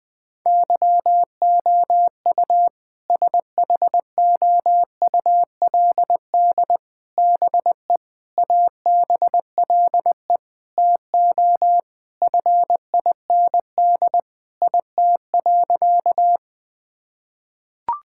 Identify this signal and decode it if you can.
Morse code, heard as a keyed tone, about 20 wpm, 700 Hz. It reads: YOU SHOULD BE ABLE TO FIND IT.
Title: Sentences with 4-Word Phrases - 20wpm